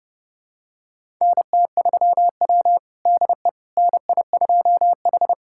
Morse code, heard as a keyed tone, about 30 wpm, 700 Hz. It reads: NT3W DE NI2H
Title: USA Callsign DE Callsign - 4 Characters - 30wpm